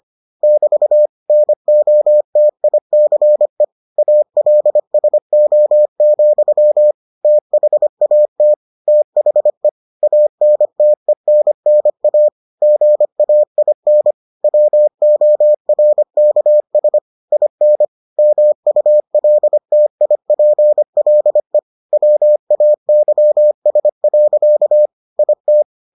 Morse code, heard as a keyed tone, about 25 wpm, 600 Hz. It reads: <BT> NOTICE ALSO, THAT THE ANTENNA GAIN WORKS IN MULTIPLE WAYS. IT